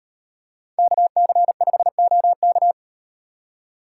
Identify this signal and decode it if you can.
Morse code, heard as a keyed tone, about 35 wpm, 700 Hz. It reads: KC5OK